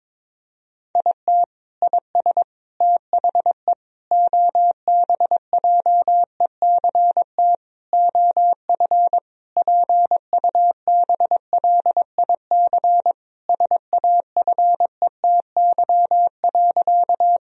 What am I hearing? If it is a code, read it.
Morse code, heard as a keyed tone, about 22 wpm, 700 Hz. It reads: IT IS THE OBJECT OF PUBLIC SAFETY.
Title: Sentences from Top 1000 Words - Review Part 4 - 22wpm